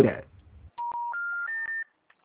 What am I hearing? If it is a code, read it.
Special information tone, with speech.